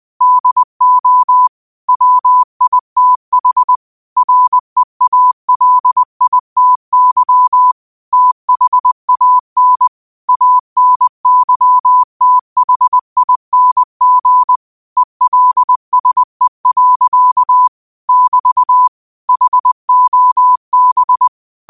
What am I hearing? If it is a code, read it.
Morse code, heard as a keyed tone, about 20 wpm, 1000 Hz. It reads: DO WITH REALITY THAN ANYTHING ELSE. <BT> HOB
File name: qotdfiles / 20wpm